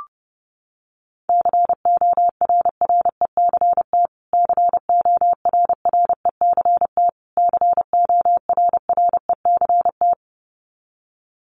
Morse code, heard as a keyed tone, about 30 wpm, 700 Hz. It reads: CORRECT CORRECT CORRECT
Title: Top 500 Words Repeated 30wpm